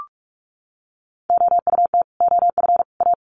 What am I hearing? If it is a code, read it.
Morse code, heard as a keyed tone, about 44 wpm, 700 Hz. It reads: OUT OF A
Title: Sets of 3 Words 45wpm